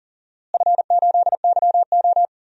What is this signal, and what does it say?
Morse code, heard as a keyed tone, about 39 wpm, 700 Hz. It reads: F8YO
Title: DXCC-100 Call Signs - 4 Characters - Repeated 3x - 40wpm